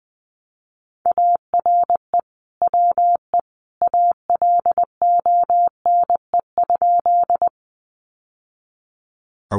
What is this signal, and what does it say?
Morse code, heard as a keyed tone, about 20 wpm, 700 Hz. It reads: ARE WE ALONE?
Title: Sentences from Top 1000 Words 20wpm